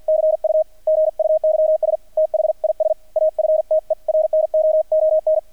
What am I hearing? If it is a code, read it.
Morse code, heard as a keyed tone, about 48 wpm, 630 Hz, with radio noise. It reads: CU GUYS THIS AFTERNOON
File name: archived / CRSnet